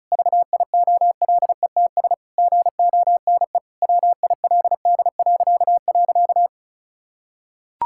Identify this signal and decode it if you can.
Morse code, heard as a keyed tone, about 36 wpm, 700 Hz. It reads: VIOLETS GONE WILD..
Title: Pangram Challenge v2 - 35wpm